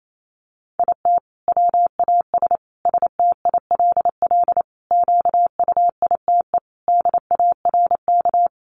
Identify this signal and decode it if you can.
Morse code, heard as a keyed tone, about 28 wpm, 700 Hz. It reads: IT WAS STILL QUITE DARK
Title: Sentences from Top 900 Words 28wpm